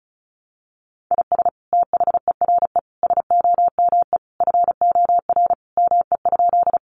Morse code, heard as a keyed tone, about 34 wpm, 700 Hz. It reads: IS THERE SOME FOR ME?
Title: Sentences from Top 200 Words 35wpm